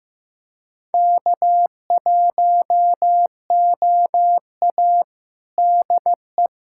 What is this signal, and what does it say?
Morse code, heard as a keyed tone, about 15 wpm, 700 Hz. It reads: K1OA DE
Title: USA Callsign DE Callsign - 4 Characters - 15wpm